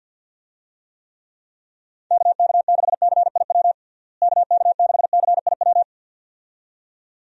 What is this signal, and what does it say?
Morse code, heard as a keyed tone, about 52 wpm, 700 Hz. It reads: KK6CIW KK6CIW
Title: USA Call Signs - 6 Characters - Repeated 2x - 50wpm